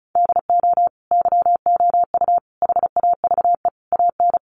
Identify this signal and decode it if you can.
Morse code, heard as a keyed tone, about 34 wpm, 700 Hz. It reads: DO YOU HAVE AN
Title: Sentences from Top 1500 Words 35wpm